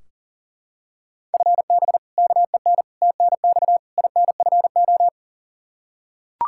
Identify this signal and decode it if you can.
Morse code, heard as a keyed tone, about 40 wpm, 700 Hz. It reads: FB KEN TNX INFO
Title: Practice QSO - Exchange 3 - More Info - 40wpm